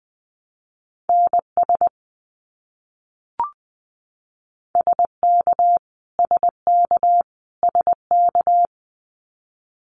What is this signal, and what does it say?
Morse code, heard as a keyed tone, about 20 wpm, 700 Hz. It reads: NS SK SK SK
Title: Canadian Provinces and Territories - Repeated 3x - 20wpm